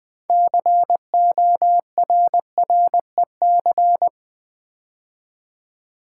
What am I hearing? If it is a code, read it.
Morse code, heard as a keyed tone, about 20 wpm, 700 Hz. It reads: CORREC
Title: Build A Word - 3 Letters At A Time - 9-Letter Words - 20wpm